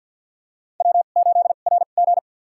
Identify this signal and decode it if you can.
Morse code, heard as a keyed tone, about 51 wpm, spaced as for 36 wpm, 700 Hz. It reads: W8RG
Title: USA Call Signs - 4 Characters - 2x Character Spacing - 50wpm